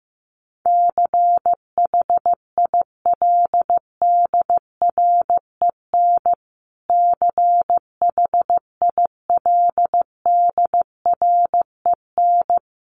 Morse code, heard as a keyed tone, about 15 wpm, 700 Hz. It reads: CHILDREN CHILDREN
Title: Top 500 Words Repeated 15wpm